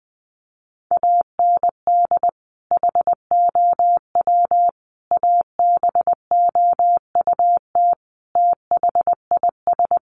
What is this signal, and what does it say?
Morse code, heard as a keyed tone, about 20 wpm, 700 Hz. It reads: AND HOW ABOUT THIS
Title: Sentences from Top 1200 Words 20wpm